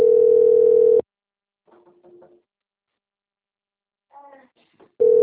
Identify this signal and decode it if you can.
Ringback tone (North America), 440 + 480 Hz.